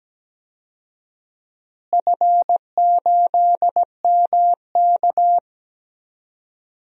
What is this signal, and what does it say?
Morse code, heard as a keyed tone, about 17 wpm, 700 Hz. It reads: F8MK
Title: DXCC-100 Call Signs - 4 Characters - Repeated 2x - 17wpm